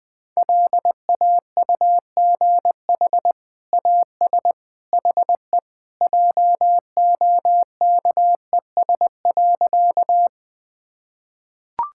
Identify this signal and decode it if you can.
Morse code, heard as a keyed tone, about 20 wpm, 700 Hz. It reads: LAUGH AS HE JOKES.
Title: Sentences with 2-Word Phrases - 20wpm